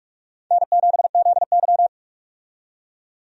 Morse code, heard as a keyed tone, about 46 wpm, 700 Hz. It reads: N7ZY
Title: USA Call Signs - 4 Characters - 45wpm